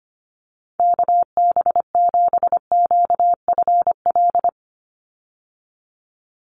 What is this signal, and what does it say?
Morse code, heard as a keyed tone, about 25 wpm, 700 Hz. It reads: KB7QFL
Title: USA Call Signs 25wpm